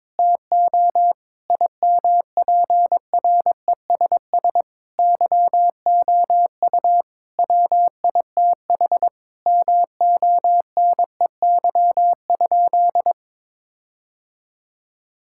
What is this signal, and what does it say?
Morse code, heard as a keyed tone, about 22 wpm, 700 Hz. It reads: TO IMPRESS YOU WITH MONEY?
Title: Sentences from Top 1600 Words 22wpm